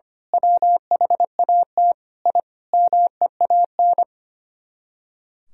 Morse code, heard as a keyed tone, about 24 wpm, 700 Hz. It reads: WHAT I MEAN
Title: Speed-Racing - Sets of 5 Words 25wpm